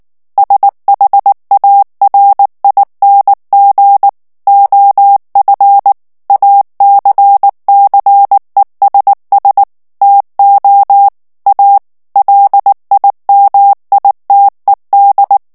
Morse code, heard as a keyed tone, about 19 wpm, 800 Hz. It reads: SHARING OF ACCESS TO A LIMITED
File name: ANGLAIS texte clair 3 vitesse 18 mots minute